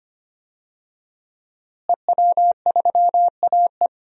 Morse code, heard as a keyed tone, about 25 wpm, 700 Hz. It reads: EW3AE
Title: DXCC-100 Call Signs - 5 Characters - Repeated 2x - 25wpm